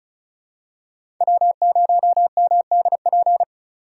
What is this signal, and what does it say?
Morse code, heard as a keyed tone, about 36 wpm, 700 Hz. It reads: W0MDP
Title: USA Call Signs - 4x Character Spacing - 35wpm